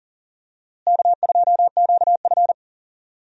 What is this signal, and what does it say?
Morse code, heard as a keyed tone, about 40 wpm, 700 Hz. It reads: K2QF
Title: USA Call Signs - 4 Characters - Repeated 2x - 40wpm